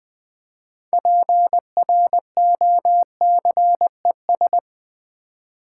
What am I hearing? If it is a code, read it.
Morse code, heard as a keyed tone, about 20 wpm, 700 Hz. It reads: PROCES
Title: Build A Word - 3 Letters At A Time - 9-Letter Words - 20wpm